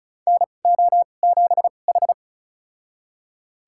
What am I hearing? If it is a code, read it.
Morse code, heard as a keyed tone, about 36 wpm, spaced as for 25 wpm, 700 Hz. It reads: NO7H
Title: USA Call Signs - 2x Character Spacing - 35wpm